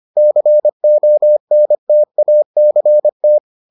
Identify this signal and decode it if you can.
Morse code, heard as a keyed tone, about 25 wpm, 600 Hz. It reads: CONTACT